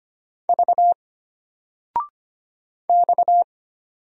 Morse code, heard as a keyed tone, about 25 wpm, 700 Hz. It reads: V X
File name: Single-Letter-Number-25wpm